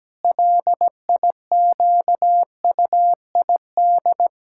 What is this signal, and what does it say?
Morse code, heard as a keyed tone, about 17 wpm, 700 Hz. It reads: LIQUID
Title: Pangram Challenge v2 - 17wpm